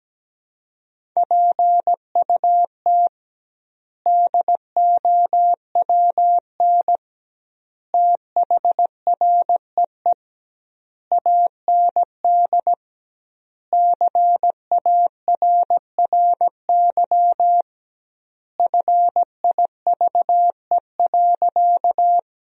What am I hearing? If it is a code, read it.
Morse code, heard as a keyed tone, about 17 wpm, 700 Hz. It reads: PUT DOWN THREE AND CARRY FIVE.
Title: 2x Word-Spacing - Sentences from Top 400 Words - 17wpm